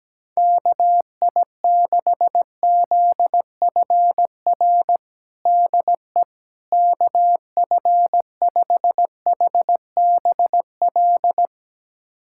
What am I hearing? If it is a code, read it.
Morse code, heard as a keyed tone, about 17 wpm, 700 Hz. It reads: KI6ZFR DE KF5HBL
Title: USA Callsign DE Callsign - 6 Characters - 17wpm